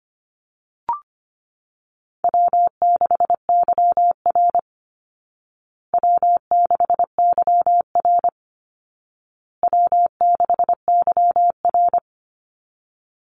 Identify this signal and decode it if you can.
Morse code, heard as a keyed tone, about 25 wpm, 700 Hz. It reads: W6YR W6YR W6YR
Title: Easy Call Signs Repeated 3x - 25wpm